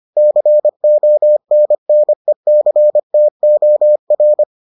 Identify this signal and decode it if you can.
Morse code, heard as a keyed tone, about 25 wpm, 600 Hz. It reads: CONNECTOR